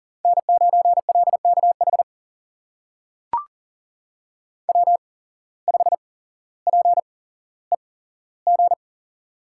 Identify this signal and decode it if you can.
Morse code, heard as a keyed tone, about 42 wpm, spaced as for 17 wpm, 700 Hz. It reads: N9LKH W5PEG
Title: USA Call Signs - 5 Characters - 8x Character Spacing - 40wpm